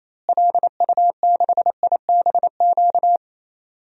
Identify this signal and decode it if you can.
Morse code, heard as a keyed tone, about 28 wpm, 700 Hz. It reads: LU6IBQ
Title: DXCC-100 Call Signs - 6 Characters - Repeated 2x - 28wpm